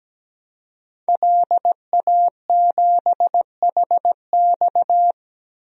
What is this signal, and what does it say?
Morse code, heard as a keyed tone, about 17 wpm, 700 Hz. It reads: LA7HX
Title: DXCC-100 Call Signs - 5 Characters - 17wpm